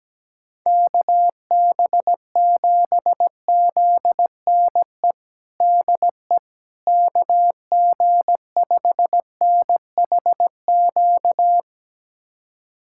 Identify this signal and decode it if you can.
Morse code, heard as a keyed tone, about 17 wpm, 700 Hz. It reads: KB7ZNE DE KG5NHQ
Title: USA Callsign DE Callsign 17wpm